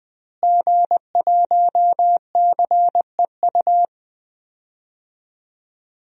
Morse code, heard as a keyed tone, about 20 wpm, 700 Hz. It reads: G1CEU
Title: DXCC-100 Call Signs - 5 Characters - Repeated 2x - 20wpm